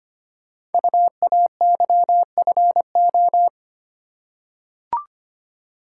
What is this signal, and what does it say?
Morse code, heard as a keyed tone, about 25 wpm, 700 Hz. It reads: UAYFO
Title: Code Groups - 5 Letters - 25wpm